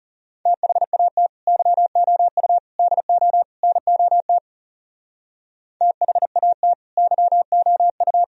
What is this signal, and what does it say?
Morse code, heard as a keyed tone, about 34 wpm, 700 Hz. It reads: THAT YOU DO NOT THAT YOU
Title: Speed-Racing - Sets of 4 Words 30wpm